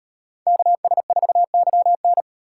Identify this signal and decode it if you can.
Morse code, heard as a keyed tone, about 34 wpm, 700 Hz. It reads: KS4YN